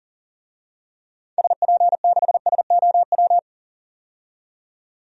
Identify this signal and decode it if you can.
Morse code, heard as a keyed tone, about 40 wpm, 700 Hz. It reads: SP6SOW
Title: DXCC-100 Call Signs - 6 Characters - 40wpm